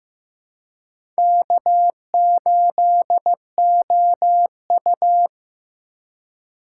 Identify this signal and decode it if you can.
Morse code, heard as a keyed tone, about 15 wpm, 700 Hz. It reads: K8OU
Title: USA Call Signs - 4 Characters - 4x Character Spacing - 15wpm